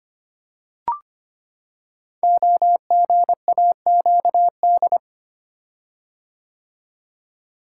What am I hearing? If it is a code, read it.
Morse code, heard as a keyed tone, about 25 wpm, 700 Hz. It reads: OGAQD
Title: Code Groups - 5 Letters - 25wpm